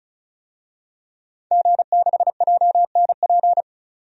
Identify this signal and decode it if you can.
Morse code, heard as a keyed tone, about 34 wpm, 700 Hz. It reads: G6JNP
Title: DXCC-100 Call Signs - 5 Characters - 35wpm